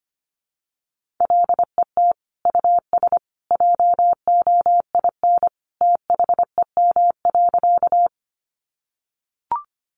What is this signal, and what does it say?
Morse code, heard as a keyed tone, about 25 wpm, 700 Hz. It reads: LET US JOIN THEM.
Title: Sentences from Top 1400 Words 25wpm